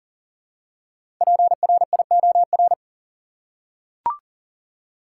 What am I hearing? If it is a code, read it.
Morse code, heard as a keyed tone, about 40 wpm, 700 Hz. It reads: PRIOR
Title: Five Letter Words - 40wpm